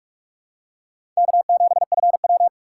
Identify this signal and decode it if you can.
Morse code, heard as a keyed tone, about 46 wpm, 700 Hz. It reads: K7FW
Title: USA Call Signs - 4 Characters - 4x Character Spacing - 45wpm